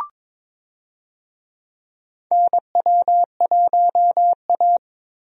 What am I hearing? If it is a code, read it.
Morse code, heard as a keyed tone, about 22 wpm, 700 Hz. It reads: NW1A
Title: USA Call Signs - 4 Characters - Repeated 3x - 22wpm